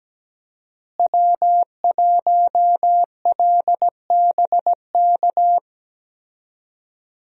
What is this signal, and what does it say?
Morse code, heard as a keyed tone, about 17 wpm, 700 Hz. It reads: W1LBK
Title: USA Call Signs - 5 Characters - Repeated 3x - 17wpm